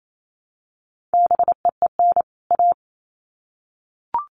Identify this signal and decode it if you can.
Morse code, heard as a keyed tone, about 28 wpm, 700 Hz. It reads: BEEN A
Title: Sets of 2 Words 28wpm